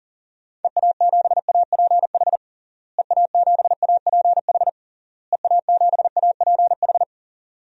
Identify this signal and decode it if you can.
Morse code, heard as a keyed tone, about 38 wpm, 700 Hz. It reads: EA7APH EA7APH EA7APH
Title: DXCC-100 Call Signs - 6 Characters - Repeated 3x - 40wpm